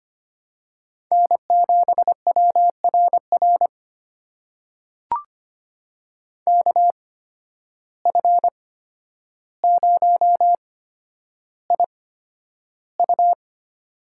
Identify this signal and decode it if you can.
Morse code, heard as a keyed tone, about 25 wpm, spaced as for 11 wpm, 700 Hz. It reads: N7WRR KF0IU
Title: USA Call Signs - 8x Character Spacing - 25wpm